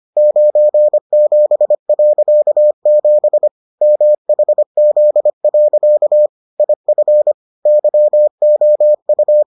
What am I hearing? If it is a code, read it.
Morse code, heard as a keyed tone, about 25 wpm, 600 Hz. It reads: 97.7 MHZ. IF YOU